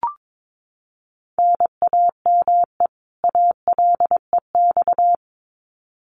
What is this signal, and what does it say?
Morse code, heard as a keyed tone, about 22 wpm, 700 Hz. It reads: NAME ALEX
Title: Common QSO Phrases 22wpm